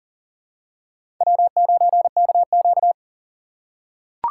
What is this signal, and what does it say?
Morse code, heard as a keyed tone, about 40 wpm, 700 Hz. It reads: W9KQ
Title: USA Call Signs - 4 Characters - 6x Character Spacing - 40wpm